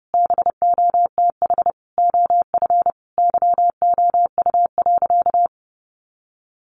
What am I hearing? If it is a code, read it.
Morse code, heard as a keyed tone, about 30 wpm, 700 Hz. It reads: BOTH OF YOU.
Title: Sentences from Top 1200 Words 30wpm